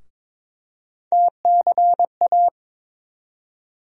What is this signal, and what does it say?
Morse code, heard as a keyed tone, about 22 wpm, 700 Hz. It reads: TCA
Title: USA Call Sign Postfixes - 3 Letters 22wpm